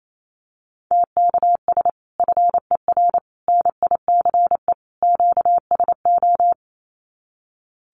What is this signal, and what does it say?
Morse code, heard as a keyed tone, about 28 wpm, 700 Hz. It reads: TKS FER NICE QSO
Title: Common QSO Phrases 28wpm